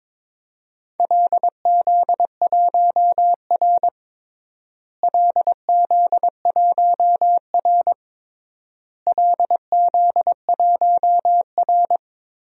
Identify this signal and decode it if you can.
Morse code, heard as a keyed tone, about 22 wpm, 700 Hz. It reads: LZ1R LZ1R LZ1R
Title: DXCC-100 Call Signs - 4 Characters - Repeated 3x - 22wpm